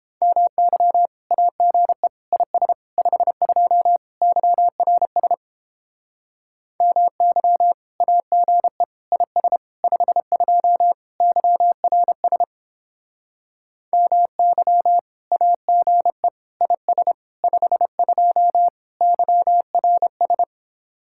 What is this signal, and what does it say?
Morse code, heard as a keyed tone, about 29 wpm, 700 Hz. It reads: MY AGE IS 52 YRS MY AGE IS 52 YRS MY AGE IS 52 YRS